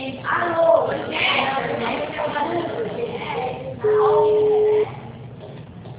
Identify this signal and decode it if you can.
Ringback tone (Europe, 425 Hz), with speech.